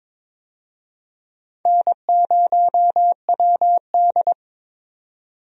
Morse code, heard as a keyed tone, about 22 wpm, 700 Hz. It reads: N0WD
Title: USA Call Signs - 4 Characters - 22wpm